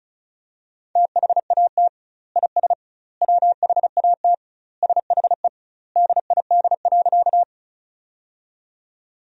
Morse code, heard as a keyed tone, about 35 wpm, 700 Hz. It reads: THAT IS WHAT SHE DID.
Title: 2x Word-Spacing - Sentences from Top 100 Words - 35wpm